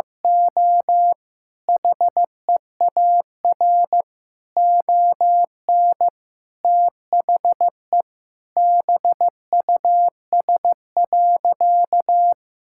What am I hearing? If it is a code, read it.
Morse code, heard as a keyed tone, about 15 wpm, 700 Hz. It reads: O HEAR ON THE BUS.
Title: Sentences from Top 1000 Words - Review Part 6 - 15wpm